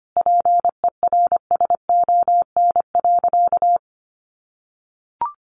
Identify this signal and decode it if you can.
Morse code, heard as a keyed tone, about 25 wpm, 700 Hz. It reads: PERSON.
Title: Sentences from Top 600 Words 25wpm